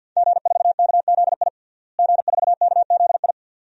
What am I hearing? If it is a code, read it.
Morse code, heard as a keyed tone, about 51 wpm, 700 Hz. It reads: G4KZI G4KZI
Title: DXCC-100 Call Signs - 5 Characters - Repeated 2x - 50wpm